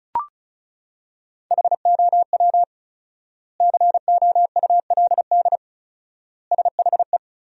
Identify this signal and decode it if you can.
Morse code, heard as a keyed tone, about 36 wpm, 700 Hz. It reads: HOW COULD SHE
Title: 4x Word-Spacing - Sentences from Top 200 Words - 35wpm